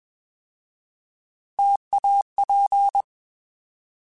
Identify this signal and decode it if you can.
Morse code, heard as a keyed tone, about 21 wpm, 800 Hz. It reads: TAP
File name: Alphabetical / Alphabetical mp3s / Ts @20